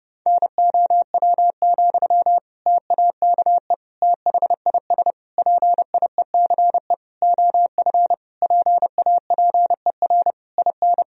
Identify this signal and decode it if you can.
Morse code, heard as a keyed tone, about 30 wpm, 700 Hz. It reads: NOW, TAKE THIS PIECE OF PAPER IN
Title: Sentences from Top 400 Words and Abbreviations 30wpm